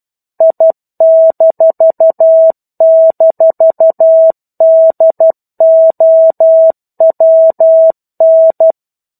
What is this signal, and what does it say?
Morse code, heard as a keyed tone, about 12 wpm, 650 Hz. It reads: I--DOWN
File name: 01-Chapter-01 12 WPM